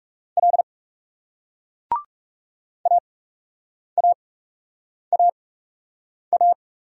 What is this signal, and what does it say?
Morse code, heard as a keyed tone, about 40 wpm, spaced as for 8 wpm, 700 Hz. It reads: L AAAA